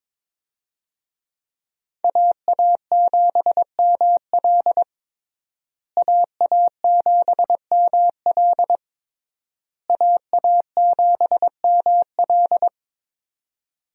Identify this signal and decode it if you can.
Morse code, heard as a keyed tone, about 22 wpm, 700 Hz. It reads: AA7ML AA7ML AA7ML
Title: USA Call Signs - 5 Characters - Repeated 3x - 22wpm